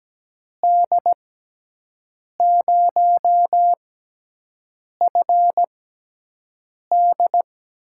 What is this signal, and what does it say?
Morse code, heard as a keyed tone, about 17 wpm, spaced as for 6 wpm, 700 Hz. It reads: D0FD
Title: USA Call Signs - 5 Characters - 6x Character Spacing - 17wpm